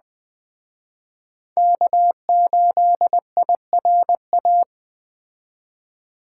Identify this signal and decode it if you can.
Morse code, heard as a keyed tone, about 20 wpm, 700 Hz. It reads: K8IRA
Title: Speed Racing - USA 1x3 Call Signs 17wpm